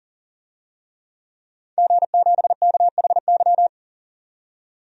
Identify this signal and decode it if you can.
Morse code, heard as a keyed tone, about 40 wpm, 700 Hz. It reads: G7KHY
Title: DXCC-100 Call Signs - 5 Characters - 40wpm